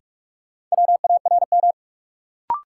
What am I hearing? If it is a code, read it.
Morse code, heard as a keyed tone, about 42 wpm, 700 Hz. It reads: WARM
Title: Sentences from Top 500 Words - New Word Drill - 45wpm